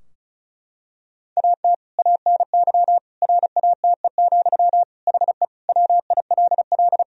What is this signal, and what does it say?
Morse code, heard as a keyed tone, about 34 wpm, 700 Hz. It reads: AT ANY RATE, HE WILL
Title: Sentences from Top 1000 Words - Review Part 4 - 35wpm